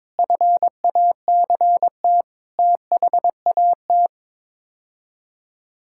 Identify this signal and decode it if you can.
Morse code, heard as a keyed tone, about 22 wpm, 700 Hz. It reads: FACT THAT
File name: practice / 22wpm / Sets of 6 Words - 22wpm